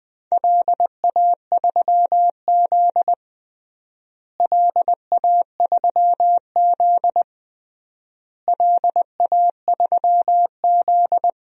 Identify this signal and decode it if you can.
Morse code, heard as a keyed tone, about 20 wpm, 700 Hz. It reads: LA3Z LA3Z LA3Z
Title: DXCC-100 Call Signs - 4 Characters - Repeated 3x - 20wpm